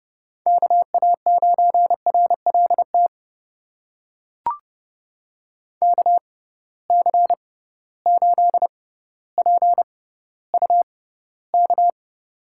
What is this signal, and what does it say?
Morse code, heard as a keyed tone, about 31 wpm, spaced as for 16 wpm, 700 Hz. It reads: KA9RLT KC8PUK
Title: USA Call Signs - 6 Characters - 6x Character Spacing - 30wpm